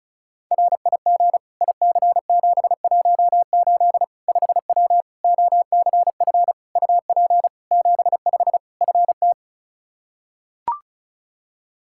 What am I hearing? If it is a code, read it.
Morse code, heard as a keyed tone, about 36 wpm, 700 Hz. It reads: RIG IC718 5W OCF UP 75 FT
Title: K1USN SST QSO Practice 35wpm